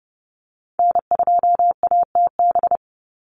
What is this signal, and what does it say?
Morse code, heard as a keyed tone, about 30 wpm, 700 Hz. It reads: N2ATB
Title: Call Signs - 30wpm